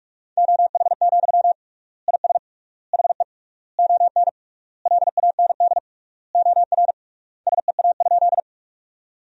Morse code, heard as a keyed tone, about 48 wpm, 700 Hz. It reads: OH, IS HE ON LAND OR SEA?
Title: 3x Word-Spacing - Sentences from Top 500 Words - 45wpm